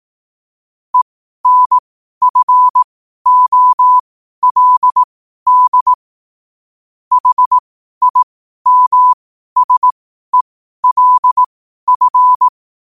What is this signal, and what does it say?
Morse code, heard as a keyed tone, about 18 wpm, spaced as for 12 wpm, 1000 Hz. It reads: ENFOLD HIMSELF
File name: qotdfiles / 12wpm